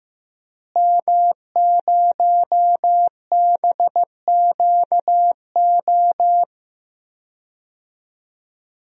Morse code, heard as a keyed tone, about 15 wpm, 700 Hz. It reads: M0BQO
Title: DXCC-100 Call Signs - 5 Characters - Repeated 2x - 15wpm